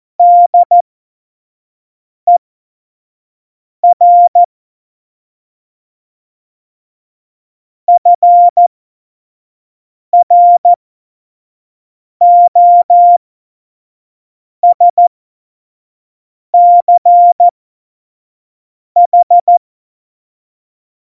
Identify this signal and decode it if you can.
Morse code, heard as a keyed tone, about 14 wpm, spaced as for 5 wpm, 700 Hz. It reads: DER FROSCH
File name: Morsekurs / Witze / Witz 10 April 2018, 25 F70